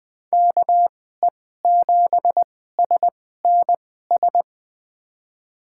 Morse code, heard as a keyed tone, about 20 wpm, spaced as for 14 wpm, 700 Hz. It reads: KE7SNS
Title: USA Call Signs - 6 Characters - 2x Character Spacing - 20wpm